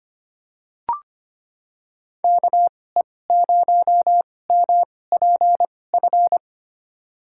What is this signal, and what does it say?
Morse code, heard as a keyed tone, about 25 wpm, spaced as for 18 wpm, 700 Hz. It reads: KE0MPF
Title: USA Call Signs - 6 Characters - 2x Character Spacing - 25wpm